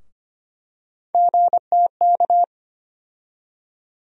Morse code, heard as a keyed tone, about 25 wpm, 700 Hz. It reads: GTK
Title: USA Call Sign Postfixes - 3 Letters 25wpm